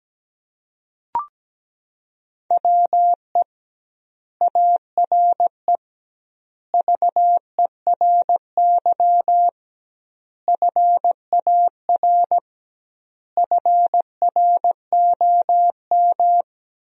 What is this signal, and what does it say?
Morse code, heard as a keyed tone, about 17 wpm, 700 Hz. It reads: WE ARE VERY FAR FROM
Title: 2x Word-Spacing - Sentences from Top 300 Words - 17wpm